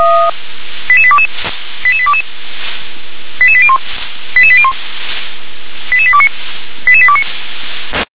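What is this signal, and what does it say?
Five-tone selective call (ZVEI), heard as quick runs of five tones, tones of about 70 ms. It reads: 88822 88822 88801 88801 88829 88829
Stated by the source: ZVEI five-tone call received by SDR.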